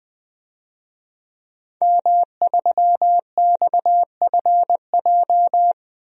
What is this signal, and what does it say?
Morse code, heard as a keyed tone, about 20 wpm, 700 Hz. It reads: M3XFJ
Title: DXCC-100 Call Signs - 5 Characters - Repeated 2x - 20wpm